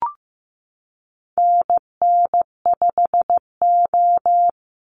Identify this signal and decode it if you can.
Morse code, heard as a keyed tone, about 15 wpm, 700 Hz. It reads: NN5O
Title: Call Signs Repeated 3x - 15wpm